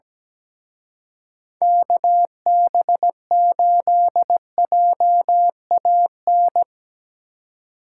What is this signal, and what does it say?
Morse code, heard as a keyed tone, about 19 wpm, 700 Hz. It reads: KB8JAN
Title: Speed Racing - USA 2x3 Call Signs 17wpm